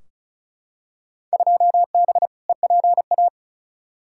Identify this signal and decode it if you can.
Morse code, heard as a keyed tone, about 36 wpm, 700 Hz. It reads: 2B EPA
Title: Field Day Exchanges Encore - 35wpm